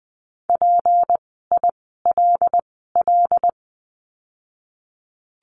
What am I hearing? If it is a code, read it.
Morse code, heard as a keyed tone, about 20 wpm, spaced as for 15 wpm, 700 Hz. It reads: PILL
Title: 153 Lesson 25f - Letter p - Words 20wpm